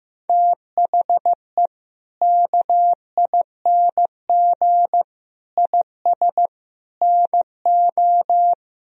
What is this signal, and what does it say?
Morse code, heard as a keyed tone, about 15 wpm, 700 Hz. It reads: THE KING IS NO
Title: Sentences from Top 1000 Words - Review Part 4 - 15wpm